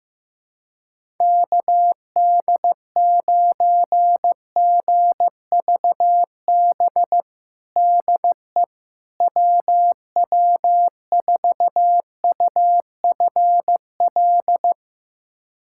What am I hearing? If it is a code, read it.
Morse code, heard as a keyed tone, about 15 wpm, 700 Hz. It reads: KD9GVB DE WW4UFL
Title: USA Callsign DE Callsign - 6 Characters - 15wpm